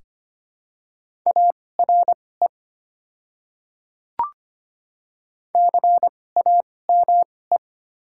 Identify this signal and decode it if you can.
Morse code, heard as a keyed tone, about 25 wpm, spaced as for 18 wpm, 700 Hz. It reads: ARE CAME
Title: 2x Character Spacing - Top 200 Words - 25wpm